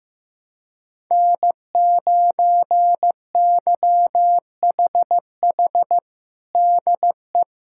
Morse code, heard as a keyed tone, about 15 wpm, 700 Hz. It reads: N9YHH DE
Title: USA Callsign DE Callsign - 5 Characters - 15wpm